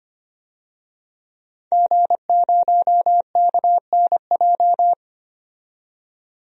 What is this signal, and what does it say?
Morse code, heard as a keyed tone, about 25 wpm, 700 Hz. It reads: G0KNJ
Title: DXCC-100 Call Signs - 5 Characters - 25wpm